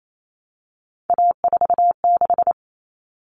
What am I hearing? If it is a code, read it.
Morse code, heard as a keyed tone, about 28 wpm, 700 Hz. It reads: A46
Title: Common Cut Numbers - 3-Digit 28wpm